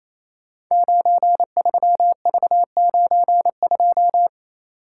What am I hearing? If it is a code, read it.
Morse code, heard as a keyed tone, about 28 wpm, 700 Hz. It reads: 93V92
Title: Code Groups - 5 Letter:Numbers - 28wpm